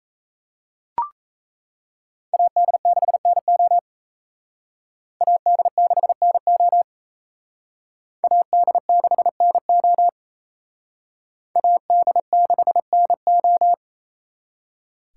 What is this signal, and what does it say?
Morse code, heard as a keyed tone, about 35 wpm, 700 Hz. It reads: AD6NO AD6NO AD6NO AD6NO
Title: Speed Racing - USA 2x2 Call Signs 28wpm